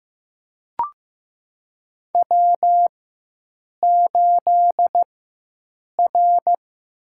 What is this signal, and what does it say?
Morse code, heard as a keyed tone, about 15 wpm, 700 Hz. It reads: W 8 R
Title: USA Call Signs - 4 Characters - 4x Character Spacing - 15wpm